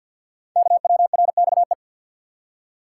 Morse code, heard as a keyed tone, about 48 wpm, 700 Hz. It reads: KWRXE
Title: Code Groups - 5 Letters - 50wpm